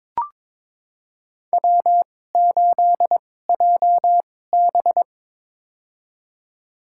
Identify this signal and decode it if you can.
Morse code, heard as a keyed tone, about 22 wpm, spaced as for 16 wpm, 700 Hz. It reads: W8JB
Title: USA Call Signs - 2x Character Spacing - 22wpm